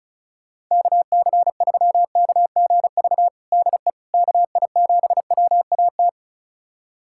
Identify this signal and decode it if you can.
Morse code, heard as a keyed tone, about 36 wpm, 700 Hz. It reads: KC3KGV DE KI7WAT
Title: USA Callsign DE Callsign 35wpm